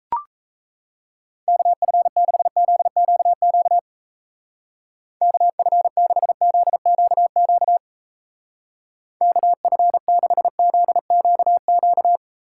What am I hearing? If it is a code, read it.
Morse code, heard as a keyed tone, about 35 wpm, 700 Hz. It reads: KF6ZQQ KF6ZQQ KF6ZQQ